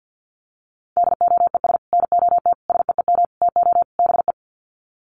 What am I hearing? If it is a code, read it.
Morse code, heard as a keyed tone, about 50 wpm, 700 Hz. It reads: DOES NOT SEEM TO BE
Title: Sets of 5 Words 50wpm